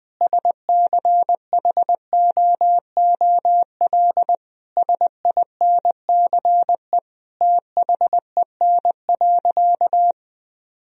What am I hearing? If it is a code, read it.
Morse code, heard as a keyed tone, about 20 wpm, 700 Hz. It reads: SCHOOL SINCE THEN.
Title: Sentences from Top 1000 Words - Review Part 1 - 20wpm